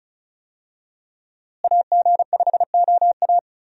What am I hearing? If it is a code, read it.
Morse code, heard as a keyed tone, about 36 wpm, 700 Hz. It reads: AG5OA
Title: USA Call Signs - 5 Characters - Repeated 2x - 35wpm